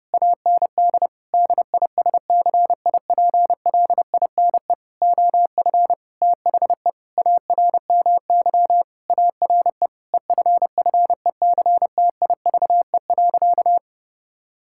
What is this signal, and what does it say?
Morse code, heard as a keyed tone, about 30 wpm, 700 Hz. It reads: AND DISCIPLINE OF THE ARMY ARE EFFECTIVE.
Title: Sentences from Top 1000 Words - Review Part 6 - 30wpm